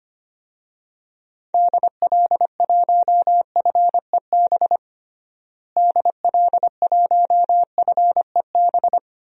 Morse code, heard as a keyed tone, about 25 wpm, 700 Hz. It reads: DL1FEB DL1FEB
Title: DXCC-100 Call Signs - 6 Characters - Repeated 3x - 25wpm